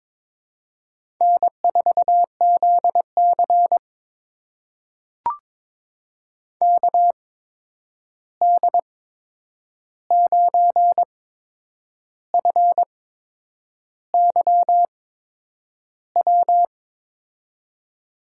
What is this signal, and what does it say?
Morse code, heard as a keyed tone, about 22 wpm, spaced as for 8 wpm, 700 Hz. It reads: N4ZC KD9FYW
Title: USA Call Signs - 8x Character Spacing - 22wpm